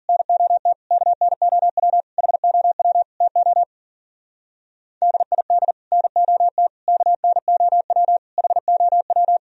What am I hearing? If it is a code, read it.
Morse code, heard as a keyed tone, about 47 wpm, 700 Hz. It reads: NOT KNOW HOW TO DID NOT KNOW HOW
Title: Speed-Racing - Sets of 5 Words 40wpm